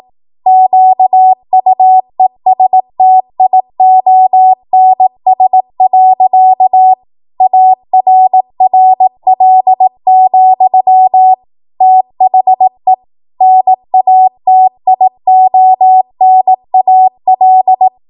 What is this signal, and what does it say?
Morse code, heard as a keyed tone, about 18 wpm, 750 Hz. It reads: QUESTIONS. ARRL, THE NATIONAL